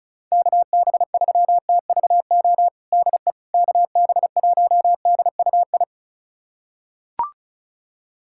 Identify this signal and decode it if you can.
Morse code, heard as a keyed tone, about 34 wpm, 700 Hz. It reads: KB3TVO DE KB1DUI
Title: USA Callsign DE Callsign - 6 Characters - 35wpm